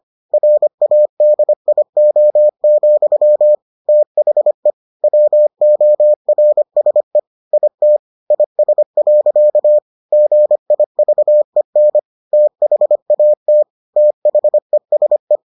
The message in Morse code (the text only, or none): RADIO, THE WORSE IT IS. GIVEN THAT THESE